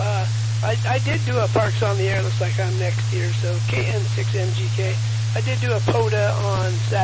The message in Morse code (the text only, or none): none